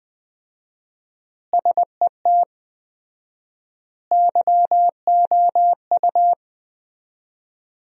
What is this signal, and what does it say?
Morse code, heard as a keyed tone, about 20 wpm, 700 Hz. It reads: SET YOU
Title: 4x Word-Spacing - Sentences from Top 200 Words - 20wpm